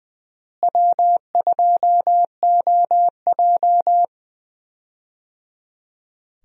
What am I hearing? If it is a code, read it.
Morse code, heard as a keyed tone, about 21 wpm, 700 Hz. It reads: W2OJ
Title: USA Call Signs - 4 Characters - Repeated 3x - 20wpm